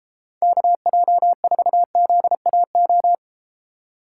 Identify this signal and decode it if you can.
Morse code, heard as a keyed tone, about 31 wpm, 700 Hz. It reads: KJ4ZAO